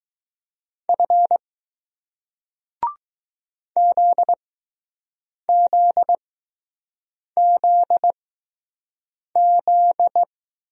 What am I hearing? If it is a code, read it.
Morse code, heard as a keyed tone, about 20 wpm, spaced as for 6 wpm, 700 Hz. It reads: F ZZZZ